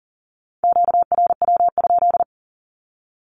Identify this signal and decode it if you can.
Morse code, heard as a keyed tone, about 40 wpm, 700 Hz. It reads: QRW?
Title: Q-Codes 40wpm